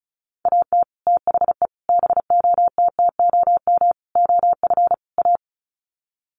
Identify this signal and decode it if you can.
Morse code, heard as a keyed tone, about 34 wpm, 700 Hz. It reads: AT THE BOTTOM OF A
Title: Sets of 5 Words Encore2 35wpm